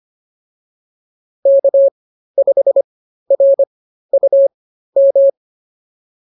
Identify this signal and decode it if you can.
Morse code, heard as a keyed tone, about 25 wpm, spaced as for 13 wpm, 550 Hz. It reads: K5RUM